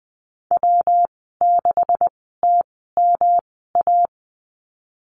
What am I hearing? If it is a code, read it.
Morse code, heard as a keyed tone, about 19 wpm, spaced as for 14 wpm, 700 Hz. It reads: W6TMA
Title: 128 Lesson 21g - Number 6 - Call Signs 20wpm